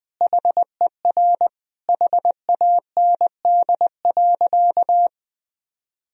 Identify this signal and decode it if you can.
Morse code, heard as a keyed tone, about 21 wpm, 700 Hz. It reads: HER HAND.
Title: Sentences from Top 1000 Words - Review Part 2 - 20wpm